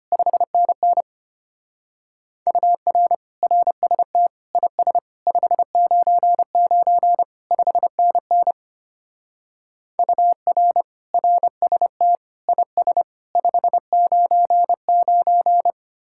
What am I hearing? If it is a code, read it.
Morse code, heard as a keyed tone, about 31 wpm, 700 Hz. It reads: 5NN UR RST IS 599 5NN UR RST IS 599
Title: Speed-Racing Common QSO Phrases Encore - 25wpm